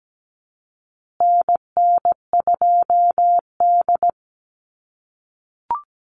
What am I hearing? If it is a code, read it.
Morse code, heard as a keyed tone, about 17 wpm, 700 Hz. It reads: NN2D
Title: USA Call Signs Repeated 2x 17wpm